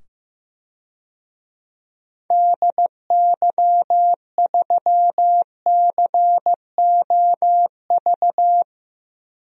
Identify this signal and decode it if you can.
Morse code, heard as a keyed tone, about 15 wpm, 700 Hz. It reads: DY3COV
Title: DXCC-100 Call Signs - 6 Characters - 15wpm